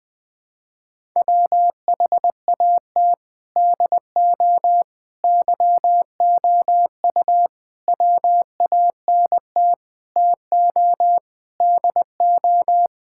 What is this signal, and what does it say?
Morse code, heard as a keyed tone, about 20 wpm, 700 Hz. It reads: WHAT DO YOU WANT TO DO
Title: Sentences with 4-Word Phrases - 20wpm